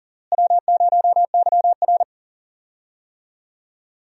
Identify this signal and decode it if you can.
Morse code, heard as a keyed tone, about 37 wpm, 700 Hz. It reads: W0YR
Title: USA Call Signs - 4 Characters - Repeated 2x - 40wpm